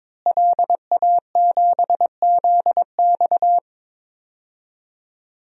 Morse code, heard as a keyed tone, about 21 wpm, 700 Hz. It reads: LA7ZX
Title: DXCC-100 Call Signs - 5 Characters - Repeated 2x - 22wpm